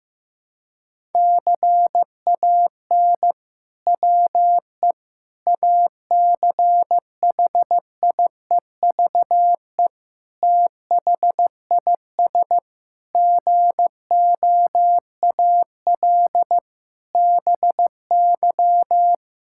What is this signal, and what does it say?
Morse code, heard as a keyed tone, about 15 wpm, 700 Hz. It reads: CAN WE ACHIEVE THIS GOAL BY
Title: Sentences from Top 1000 Words - Review Part 5 - 15wpm